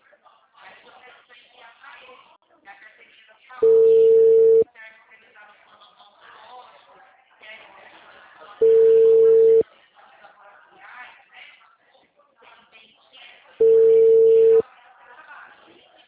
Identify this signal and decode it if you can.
Ringback tone (China, 450 Hz), with speech.